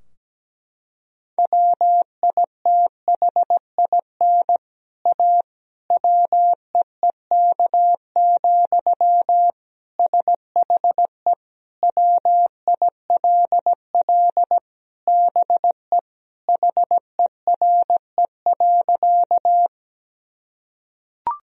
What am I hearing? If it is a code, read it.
Morse code, heard as a keyed tone, about 17 wpm, 700 Hz. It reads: WITHIN A WEEK, SHE WILL BE HERE.
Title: Sentences from Top 1000 Words - Review Part 4 - 17wpm